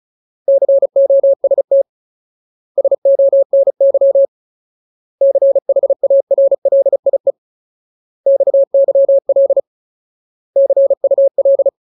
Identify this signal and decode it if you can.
Morse code, heard as a keyed tone, about 34 wpm, 550 Hz. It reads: COST SONY CHARLIE XYL CUL
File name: Ultimate Word List / 35WPM Ultimate